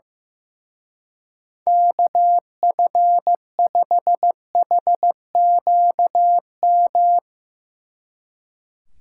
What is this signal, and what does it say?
Morse code, heard as a keyed tone, about 16 wpm, 700 Hz. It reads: KF5HQM
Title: Speed Racing - USA 2x3 Call Signs 15wpm